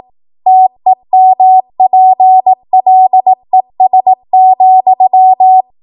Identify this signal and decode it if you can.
Morse code, heard as a keyed tone, about 18 wpm, 750 Hz. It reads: TEMPLES,